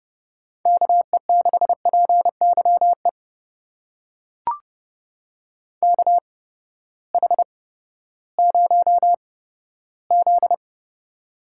Morse code, heard as a keyed tone, about 30 wpm, spaced as for 15 wpm, 700 Hz. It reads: KE6PYE KH0Z
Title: USA Call Signs - 8x Character Spacing - 30wpm